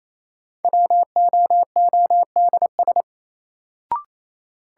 Morse code, heard as a keyed tone, about 28 wpm, 700 Hz. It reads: WOODS
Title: Sentences from Top 1000 Words - Review Part 2 - New Word Drill - 28wpm